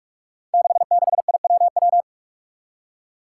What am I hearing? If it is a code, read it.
Morse code, heard as a keyed tone, about 44 wpm, 700 Hz. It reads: 66IWW
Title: Five Challenging Letter-Number Combinations 45wpm